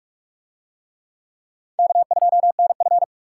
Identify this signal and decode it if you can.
Morse code, heard as a keyed tone, about 41 wpm, 700 Hz. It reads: K2NF